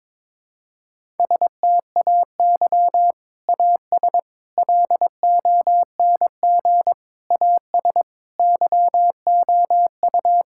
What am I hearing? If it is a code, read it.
Morse code, heard as a keyed tone, about 21 wpm, 700 Hz. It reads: STAY AS LONG AS YOU
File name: practice / 22wpm / Sentences with 3-Word Phrases - 22wpm